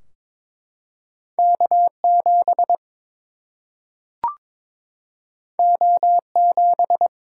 Morse code, heard as a keyed tone, about 22 wpm, 700 Hz. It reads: K 7 O 7
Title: Single-Letter Single-Number 22wpm